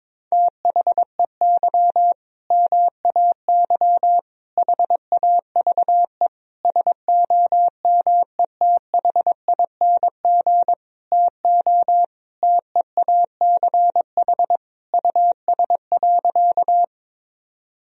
Morse code, heard as a keyed tone, about 22 wpm, 700 Hz. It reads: THEY MAY HAVE SOMETHING TO TEACH US.
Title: Sentences from Top 1000 Words - Review Part 2 - 22wpm